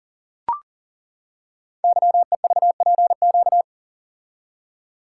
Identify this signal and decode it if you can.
Morse code, heard as a keyed tone, about 38 wpm, 700 Hz. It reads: YEVPQ
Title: Code Groups - 5 Letters - 40wpm